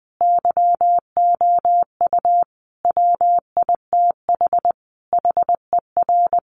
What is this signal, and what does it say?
Morse code, heard as a keyed tone, about 20 wpm, 700 Hz. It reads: YOU WITH HER
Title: Sentences from Top 1500 Words 20wpm